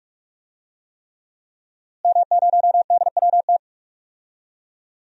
Morse code, heard as a keyed tone, about 42 wpm, 700 Hz. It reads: M0DWT
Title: DXCC-100 Call Signs - 5 Characters - Repeated 3x - 45wpm